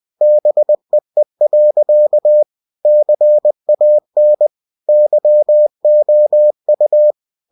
Morse code, heard as a keyed tone, about 20 wpm, 600 Hz. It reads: BEE. CAN YOU